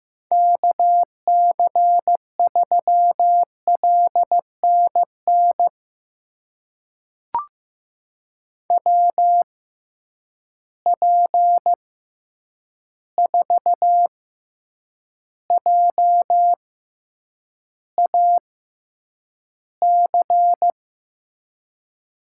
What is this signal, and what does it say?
Morse code, heard as a keyed tone, about 15 wpm, spaced as for 8 wpm, 700 Hz. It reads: KC3LNN WP4JAC
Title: USA Call Signs - 6 Characters - 6x Character Spacing - 15wpm